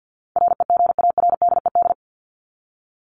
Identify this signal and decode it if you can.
Morse code, heard as a keyed tone, about 50 wpm, 700 Hz. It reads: REGARDED
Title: ED Suffix - Top 100 Words with Base Words 50wpm